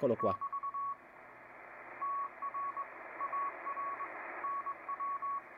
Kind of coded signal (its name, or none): Morse code